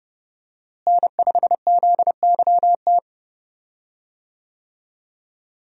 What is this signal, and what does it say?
Morse code, heard as a keyed tone, about 30 wpm, 700 Hz. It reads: N5ZYT
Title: USA Call Signs - 5 Characters - Repeated 2x - 30wpm